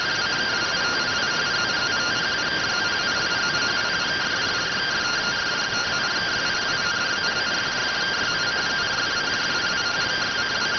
Two-tone teleprinter signal, about 170 Hz shift.